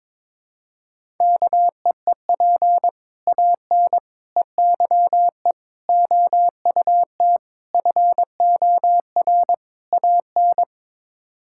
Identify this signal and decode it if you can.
Morse code, heard as a keyed tone, about 21 wpm, 700 Hz. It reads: KEEP AN EYE OUT FOR AN
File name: practice / 22wpm / Sets of 6 Words - 22wpm